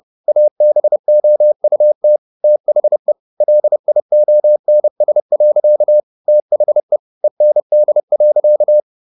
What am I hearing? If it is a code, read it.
Morse code, heard as a keyed tone, about 30 wpm, 600 Hz. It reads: ABOUT THE LIONS. THE END.